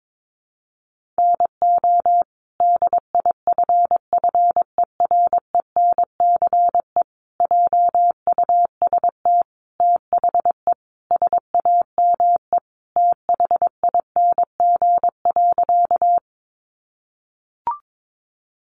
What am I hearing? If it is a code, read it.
Morse code, heard as a keyed tone, about 22 wpm, 700 Hz. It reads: NO DIFFERENCE JUST THE SAME THING.
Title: Sentences from Top 800 Words 22wpm